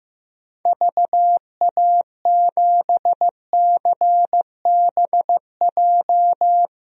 Morse code, heard as a keyed tone, about 16 wpm, 700 Hz. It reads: VA7CBJ
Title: DXCC-100 Call Signs - 6 Characters - Repeated 3x - 15wpm